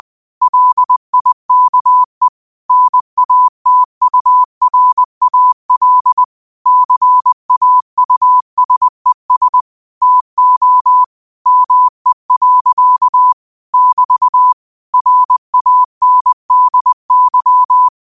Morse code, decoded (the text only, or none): LIKE NATURAL CAUSES TO ME. <BT> RANDY